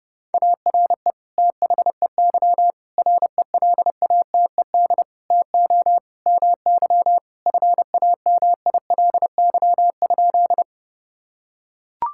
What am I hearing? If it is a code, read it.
Morse code, heard as a keyed tone, about 30 wpm, 700 Hz. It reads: ARE THEY RELATED TO MY FAMILY?
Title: Sentences from Top 1000 Words - Review Part 5 - 30wpm